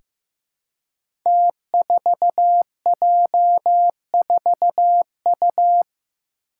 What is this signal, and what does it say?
Morse code, heard as a keyed tone, about 15 wpm, 700 Hz. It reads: T4J4U
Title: Five Challenging Letter-Number Combinations 15wpm